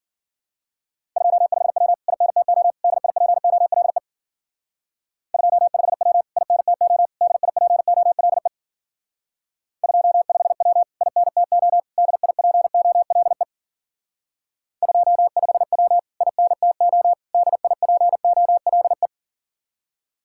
Morse code, heard as a keyed tone, about 50 wpm, 700 Hz. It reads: 25W INTO DIPOLE 25W INTO DIPOLE 25W INTO DIPOLE 25W INTO DIPOLE
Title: Speed-Racing Common QSO Phrases - 40wpm